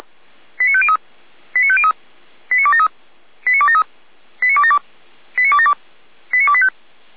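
Five-tone selective call (ZVEI), heard as quick runs of five tones, tones of about 70 ms. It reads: 89503 89503 89273 89273 89272 89272 89285
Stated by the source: ZVEI five-tone selective call received on SDR.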